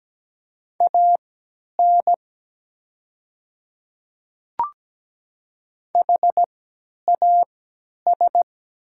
Morse code, heard as a keyed tone, about 17 wpm, spaced as for 10 wpm, 700 Hz. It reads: AN HAS
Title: 3x Character Spacing - Top 200 Words - 17wpm